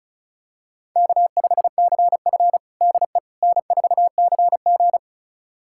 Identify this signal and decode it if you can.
Morse code, heard as a keyed tone, about 36 wpm, 700 Hz. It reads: K5CF DE N4CG
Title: USA Callsign DE Callsign - 4 Characters - 35wpm